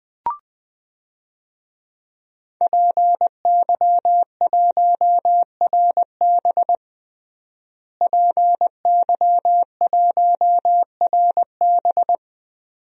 Morse code, heard as a keyed tone, about 20 wpm, 700 Hz. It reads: PY1RB PY1RB
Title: DXCC-100 Call Signs - 5 Characters - Repeated 2x - 20wpm